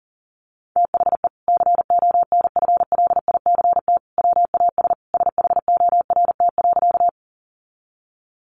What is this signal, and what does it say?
Morse code, heard as a keyed tone, about 40 wpm, 700 Hz. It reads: THE CONFLICT WAS SHORT.
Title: Sentences from Top 1300 Words 40wpm